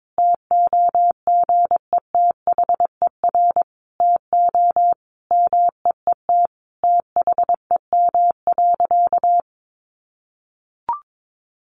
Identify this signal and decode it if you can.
Morse code, heard as a keyed tone, about 22 wpm, 700 Hz. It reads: TOGETHER TO MEET THEM.
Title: Sentences from Top 1100 Words 22wpm